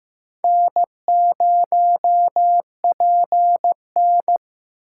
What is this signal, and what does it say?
Morse code, heard as a keyed tone, about 15 wpm, 700 Hz. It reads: N0PN
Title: USA Call Signs - 2x Character Spacing - 15wpm